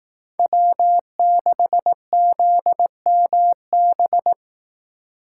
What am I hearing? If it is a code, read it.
Morse code, heard as a keyed tone, about 18 wpm, 700 Hz. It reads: W6ZMB
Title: Speed Racing - USA 1x3 Call Signs 15wpm